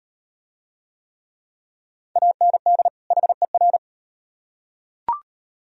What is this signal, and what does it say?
Morse code, heard as a keyed tone, about 38 wpm, 700 Hz. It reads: AND HER